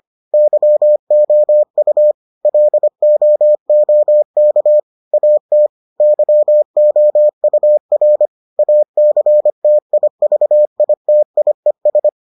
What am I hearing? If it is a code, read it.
Morse code, heard as a keyed tone, about 25 wpm, 600 Hz. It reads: YOU LOOK AT YOUR ACTIVITIES